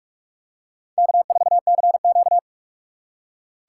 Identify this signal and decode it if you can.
Morse code, heard as a keyed tone, about 46 wpm, 700 Hz. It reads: K4CQ
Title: USA Call Signs - 4 Characters - 2x Character Spacing - 45wpm